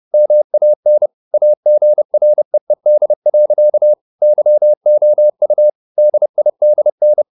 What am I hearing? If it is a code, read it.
Morse code, heard as a keyed tone, about 30 wpm, 600 Hz. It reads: MAN AGREED. YOU DIDN
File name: the old man and the sea / the old man and the sea 1 30wpm